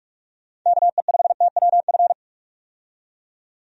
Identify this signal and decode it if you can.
Morse code, heard as a keyed tone, about 43 wpm, 700 Hz. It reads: KE5TWF
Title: USA Call Signs - 6 Characters - 45wpm